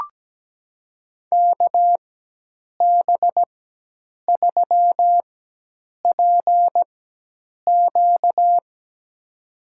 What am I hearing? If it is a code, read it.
Morse code, heard as a keyed tone, about 17 wpm, spaced as for 8 wpm, 700 Hz. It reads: KB3PQ
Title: USA Call Signs - 5 Characters - 4x Character Spacing - 17wpm